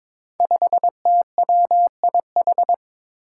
Morse code, heard as a keyed tone, about 22 wpm, 700 Hz. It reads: 5TWIH
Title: Five Challenging Letter-Number Combinations 22wpm